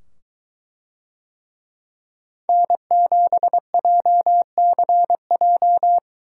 Morse code, heard as a keyed tone, about 21 wpm, 700 Hz. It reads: N7JCJ